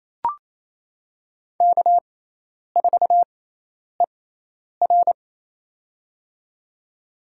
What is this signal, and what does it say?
Morse code, heard as a keyed tone, about 28 wpm, spaced as for 10 wpm, 700 Hz. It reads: K4ER
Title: USA Call Signs - 4 Characters - 6x Character Spacing - 28wpm